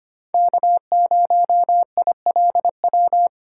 Morse code, heard as a keyed tone, about 30 wpm, 700 Hz. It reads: K0ILW
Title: Speed Racing - USA Call Signs 25wpm